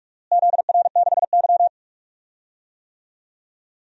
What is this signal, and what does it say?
Morse code, heard as a keyed tone, about 46 wpm, 700 Hz. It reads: ZR6Y
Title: DXCC-100 Call Signs - 4 Characters - Repeated 2x - 45wpm